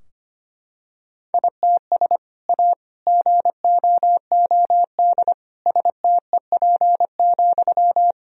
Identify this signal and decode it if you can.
Morse code, heard as a keyed tone, about 25 wpm, 700 Hz. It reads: ITS A GOOD STEP,
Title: Sentences from Top 1000 Words - Review Part 2 - 25wpm